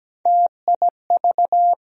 Morse code, heard as a keyed tone, about 17 wpm, 700 Hz. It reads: TIV
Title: USA Call Sign Postfixes - 3 Letters 17wpm